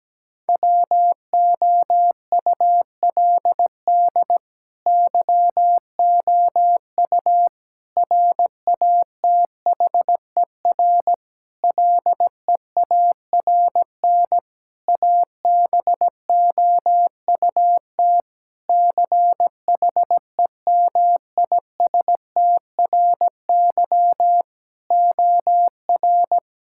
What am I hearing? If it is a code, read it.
Morse code, heard as a keyed tone, about 17 wpm, 700 Hz. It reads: WOULD YOU RATHER LEARN ABOUT CHEMISTRY OR
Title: Sentences from Top 1000 Words - Review Part 6 - 17wpm